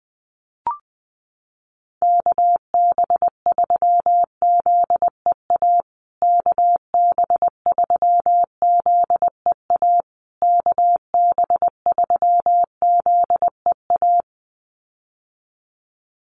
Morse code, heard as a keyed tone, about 20 wpm, 700 Hz. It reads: KB3ZEA KB3ZEA KB3ZEA
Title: USA Call Signs Repeated 3x 20wpm